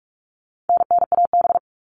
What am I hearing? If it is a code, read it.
Morse code, heard as a keyed tone, about 44 wpm, 700 Hz. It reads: NNAB
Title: All Cut Numbers - 4-Digit 45wpm